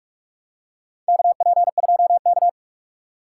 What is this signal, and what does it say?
Morse code, heard as a keyed tone, about 44 wpm, 700 Hz. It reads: KP2K
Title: USA Call Signs - 4 Characters - 6x Character Spacing - 45wpm